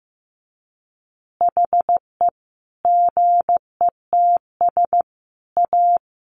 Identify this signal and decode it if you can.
Morse code, heard as a keyed tone, about 15 wpm, 700 Hz. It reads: HE GETS A
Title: Additional Easy Phrases 15wpm